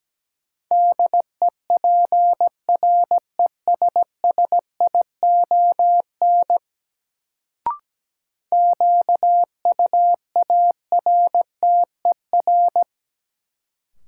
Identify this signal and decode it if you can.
Morse code, heard as a keyed tone, about 17 wpm, 700 Hz. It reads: DEPRESSION QUARTER
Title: Sentences from Top 1300 Words - New Word Drill - 17wpm